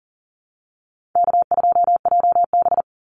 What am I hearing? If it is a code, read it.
Morse code, heard as a keyed tone, about 39 wpm, 700 Hz. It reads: K2JB
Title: NA SOTA Activator Callsigns 40wpm